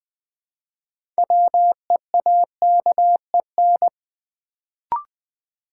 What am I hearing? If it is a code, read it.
Morse code, heard as a keyed tone, about 20 wpm, 690 Hz. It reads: WEAKEN
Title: Build A Word - 2 Letters At A Time - 6-Letter Words - 20wpm